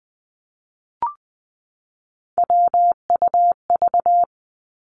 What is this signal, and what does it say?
Morse code, heard as a keyed tone, about 20 wpm, 700 Hz. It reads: WUV
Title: Three Challenging Letter-Number Combinations 20wpm